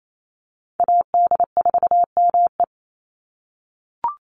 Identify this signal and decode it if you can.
Morse code, heard as a keyed tone, about 28 wpm, 700 Hz. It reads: AD4ME
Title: USA Call Signs 28wpm